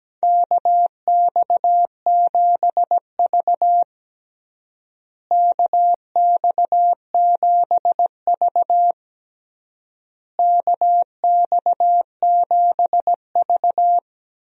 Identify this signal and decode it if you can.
Morse code, heard as a keyed tone, about 17 wpm, 700 Hz. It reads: KX7V KX7V KX7V
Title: USA Call Signs - 4 Characters - Repeated 3x - 17wpm